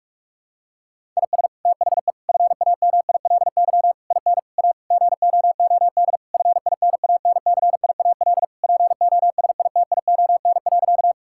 Morse code, heard as a keyed tone, about 46 wpm, 700 Hz. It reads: IS THE FAMILY IN A GOOD FINANCIAL POSITION.
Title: Sentences from Top 1000 Words - Review Part 5 - 45wpm